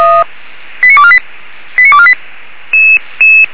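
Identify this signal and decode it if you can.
Five-tone selective call (ZVEI), heard as quick runs of five tones, tones of about 70 ms. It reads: 89248 89248
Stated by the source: Five-tone ZVEI call received on SDR, fire brigade Feuerwehr Orlamünde.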